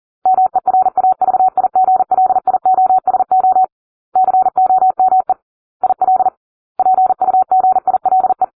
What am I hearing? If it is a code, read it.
Morse code, heard as a keyed tone, about 40 wpm, 750 Hz. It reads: MERAVIGLIOSO COME IL PUGILE